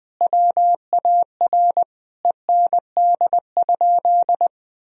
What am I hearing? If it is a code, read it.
Morse code, heard as a keyed tone, about 20 wpm, 700 Hz. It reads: WAR END?
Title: Sentences from Top 1000 Words - Review Part 2 - 20wpm